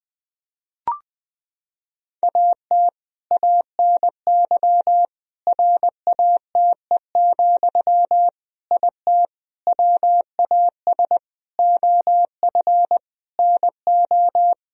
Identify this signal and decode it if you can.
Morse code, heard as a keyed tone, about 21 wpm, 700 Hz. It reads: AT ANY RATE, IT WAS OF NO
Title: Sentences from Top 1000 Words - Review Part 4 - 20wpm